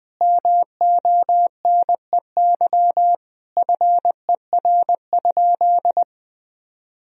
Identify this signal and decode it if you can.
Morse code, heard as a keyed tone, about 20 wpm, 700 Hz. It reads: MONEY FER?
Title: Sentences from Top 400 Words and Abbreviations 20wpm